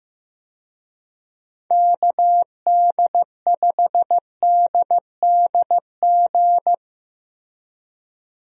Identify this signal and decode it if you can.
Morse code, heard as a keyed tone, about 16 wpm, 700 Hz. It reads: KD5DDG
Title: USA Call Signs - 6 Characters - Repeated 2x - 15wpm